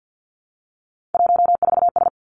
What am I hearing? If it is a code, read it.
Morse code, heard as a keyed tone, about 50 wpm, 700 Hz. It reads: 14S
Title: Three Challenging Letter-Number Combinations 50wpm